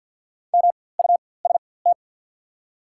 Morse code, heard as a keyed tone, about 53 wpm, spaced as for 24 wpm, 700 Hz. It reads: MUST
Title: 4x Character Spacing - Top 500 Words - 50wpm